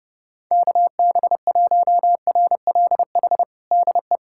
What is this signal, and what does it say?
Morse code, heard as a keyed tone, about 30 wpm, 700 Hz. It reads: KB1RLH DE
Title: USA Callsign DE Callsign 30wpm